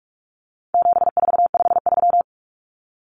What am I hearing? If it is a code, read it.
Morse code, heard as a keyed tone, about 44 wpm, 700 Hz. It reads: 7453
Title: Numbers - 4-Digit 45wpm